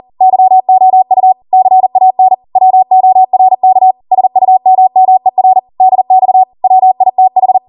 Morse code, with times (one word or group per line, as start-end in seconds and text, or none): 0.2-1.3 YOU
1.5-2.3 CAN
2.6-3.9 WORK
4.1-5.6 SUMMER
5.8-6.4 DX
6.6-7.6 WITH